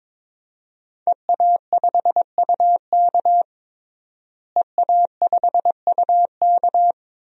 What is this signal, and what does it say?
Morse code, heard as a keyed tone, about 21 wpm, 700 Hz. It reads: EA5UK EA5UK
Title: DXCC-100 Call Signs - 5 Characters - Repeated 3x - 22wpm